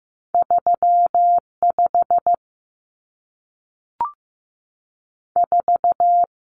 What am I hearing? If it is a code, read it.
Morse code, heard as a keyed tone, about 15 wpm, spaced as for 3 wpm, 700 Hz. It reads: 3 54
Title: Numbers - 2-Digit 15wpm